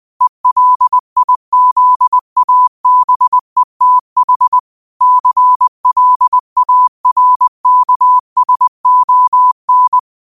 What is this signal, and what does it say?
Morse code, heard as a keyed tone, about 20 wpm, 1000 Hz. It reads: ELIZABETH CLARKSON